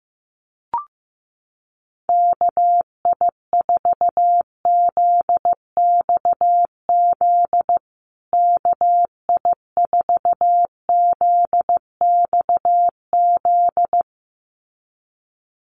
Morse code, heard as a keyed tone, about 14 wpm, 700 Hz. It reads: KI4ZXZ KI4ZXZ
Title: USA Call Signs Repeated 2x 15wpm